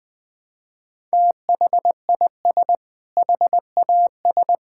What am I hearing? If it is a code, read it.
Morse code, heard as a keyed tone, about 21 wpm, 700 Hz. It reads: THIS HAS
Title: Sentences with 6-Word Phrases - 20wpm